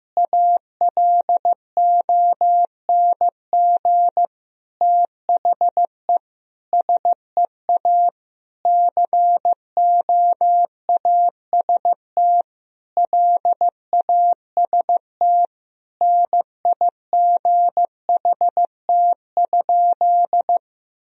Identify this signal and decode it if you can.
Morse code, heard as a keyed tone, about 15 wpm, 700 Hz. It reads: ALONG THE SEA COAST LAST NIGHT?
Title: Sentences from Top 1000 Words - Review Part 4 - 15wpm